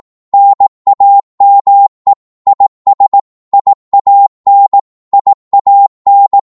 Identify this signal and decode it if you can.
Morse code, heard as a keyed tone, about 18 wpm, 800 Hz. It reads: NAME IS IAN IAN
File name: Morse / qso-18wpm